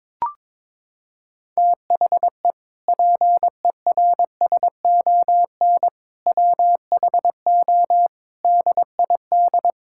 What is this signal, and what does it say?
Morse code, heard as a keyed tone, about 22 wpm, 700 Hz. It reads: THE PERSON WHO DID
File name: Sentences with 7-Word Phrases - 22wpm